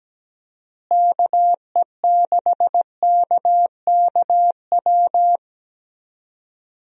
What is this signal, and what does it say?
Morse code, heard as a keyed tone, about 17 wpm, 700 Hz. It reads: KE6KKW
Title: USA Call Signs - 6 Characters - Repeated 3x - 17wpm